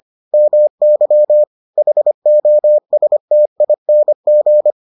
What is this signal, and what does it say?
Morse code, heard as a keyed tone, about 25 wpm, 600 Hz. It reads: MY HOSTING